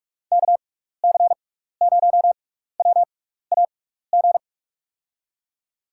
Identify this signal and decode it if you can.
Morse code, heard as a keyed tone, about 44 wpm, spaced as for 15 wpm, 700 Hz. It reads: KC0WAG
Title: USA Call Signs - 6 Characters - 6x Character Spacing - 45wpm